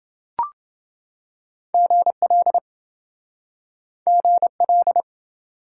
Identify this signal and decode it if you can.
Morse code, heard as a keyed tone, about 30 wpm, 700 Hz. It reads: GL GL